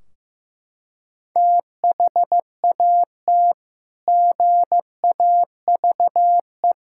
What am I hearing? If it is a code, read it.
Morse code, heard as a keyed tone, about 15 wpm, 700 Hz. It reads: THAT GAVE
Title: Sentences from Top 1600 Words 15wpm